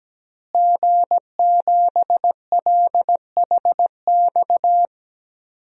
Morse code, heard as a keyed tone, about 17 wpm, 700 Hz. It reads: G7LHX
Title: DXCC-100 Call Signs - 5 Characters - Repeated 3x - 17wpm